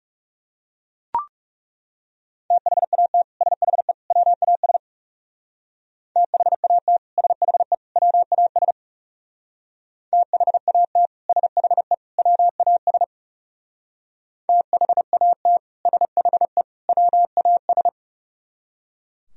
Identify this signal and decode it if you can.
Morse code, heard as a keyed tone, about 36 wpm, 700 Hz. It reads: THAT SHE WAS THAT SHE WAS THAT SHE WAS THAT SHE WAS
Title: Speed-Racing - Sets of 3 Words 30wpm